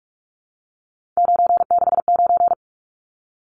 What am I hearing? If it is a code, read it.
Morse code, heard as a keyed tone, about 44 wpm, 700 Hz. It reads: 969
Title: Numbers - 3-Digit 45wpm